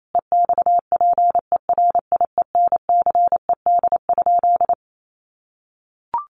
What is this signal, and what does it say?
Morse code, heard as a keyed tone, about 28 wpm, 700 Hz. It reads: EXPERIENCED?
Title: Sentences from Top 1100 Words 28wpm